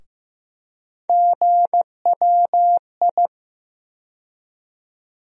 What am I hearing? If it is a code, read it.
Morse code, heard as a keyed tone, about 15 wpm, 700 Hz. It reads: GWI
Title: USA Call Sign Postfixes - 3 Letters 15wpm